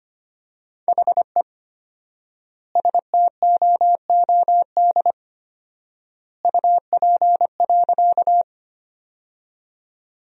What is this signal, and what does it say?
Morse code, heard as a keyed tone, about 25 wpm, 700 Hz. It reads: HE STOOD UP.
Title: 4x Word-Spacing - Sentences from Top 500 Words - 25wpm